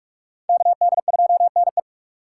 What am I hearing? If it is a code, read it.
Morse code, heard as a keyed tone, about 42 wpm, 700 Hz. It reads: KD2NE